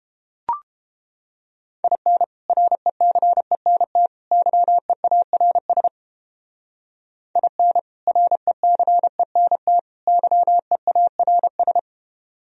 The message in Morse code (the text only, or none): IN RECENT YEARS IN RECENT YEARS